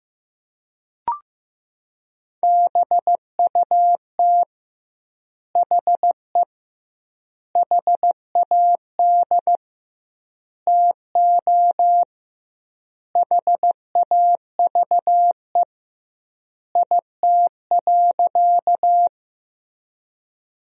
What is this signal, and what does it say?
Morse code, heard as a keyed tone, about 15 wpm, 700 Hz. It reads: BUT HE HAD TO HAVE IT.
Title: 2x Word-Spacing - Sentences from Top 100 Words - 15wpm